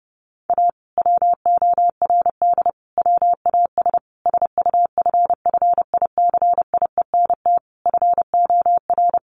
Morse code, heard as a keyed tone, about 30 wpm, 700 Hz. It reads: A WORD WAS SUFFICIENT FOR
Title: Sentences from Top 1400 Words 30wpm